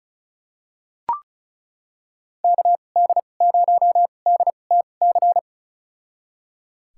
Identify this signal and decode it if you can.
Morse code, heard as a keyed tone, about 36 wpm, spaced as for 25 wpm, 700 Hz. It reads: KD0DTC
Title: USA Call Signs - 6 Characters - 2x Character Spacing - 35wpm